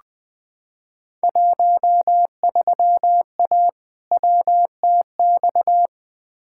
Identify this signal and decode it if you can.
Morse code, heard as a keyed tone, about 20 wpm, 700 Hz. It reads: 13A WTX
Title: Field Day Exchanges Encore - 20wpm